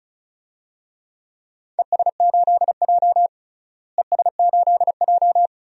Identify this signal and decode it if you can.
Morse code, heard as a keyed tone, about 34 wpm, 700 Hz. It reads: ES8J ES8J
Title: DXCC-100 Call Signs - 4 Characters - Repeated 2x - 35wpm